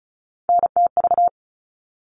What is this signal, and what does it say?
Morse code, heard as a keyed tone, about 34 wpm, 700 Hz. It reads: NTV
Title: All Cut Numbers - 3-Digit 35wpm